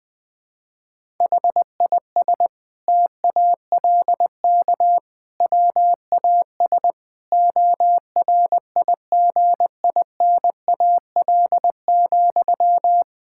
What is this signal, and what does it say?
Morse code, heard as a keyed tone, about 20 wpm, 700 Hz. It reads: HIS TALK WAS ORIGINAL,
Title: Sentences from Top 1000 Words - Review Part 6 - 20wpm